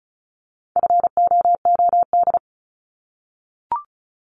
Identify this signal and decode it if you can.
Morse code, heard as a keyed tone, about 34 wpm, 700 Hz. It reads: FOOD
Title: Top 500 Words 35wpm